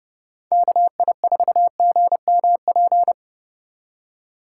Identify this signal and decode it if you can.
Morse code, heard as a keyed tone, about 31 wpm, 700 Hz. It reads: KI4GMP
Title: Speed Racing - USA Call Signs 25wpm